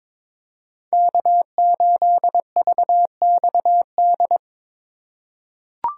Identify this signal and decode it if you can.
Morse code, heard as a keyed tone, about 21 wpm, 700 Hz. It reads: K8VXD
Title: USA Call Signs - 5 Characters - 4x Character Spacing - 22wpm